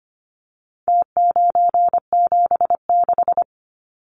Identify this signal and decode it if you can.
Morse code, heard as a keyed tone, about 25 wpm, 700 Hz. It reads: T976
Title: Numbers - Contest Style - 4-Digit 25wpm